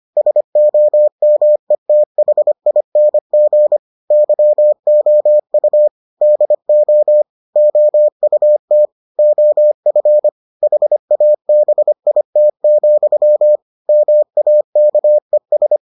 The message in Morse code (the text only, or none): SOMETHING YOU DO OUT OF HABIT, MAKES